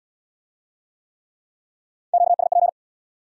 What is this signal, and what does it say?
Morse code, heard as a keyed tone, about 72 wpm, 700 Hz. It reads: QSP